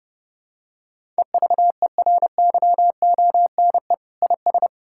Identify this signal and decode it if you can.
Morse code, heard as a keyed tone, about 30 wpm, 700 Hz. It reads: EVERYONE IS
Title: Sentences from Top 1000 Words - Review Part 5 - 30wpm